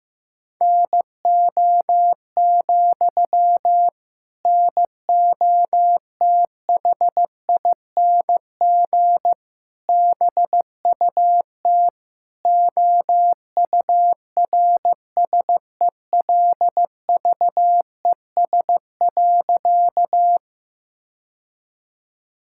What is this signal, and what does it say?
Morse code, heard as a keyed tone, about 16 wpm, 700 Hz. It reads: NO, NOTHING BUT OURSELVES.
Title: Sentences from Top 1600 Words 15wpm